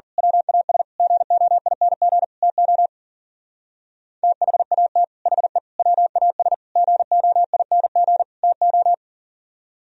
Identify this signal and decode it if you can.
Morse code, heard as a keyed tone, about 44 wpm, 700 Hz. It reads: WAS GOING TO THAT HE WAS GOING TO
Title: Speed-Racing - Sets of 5 Words 40wpm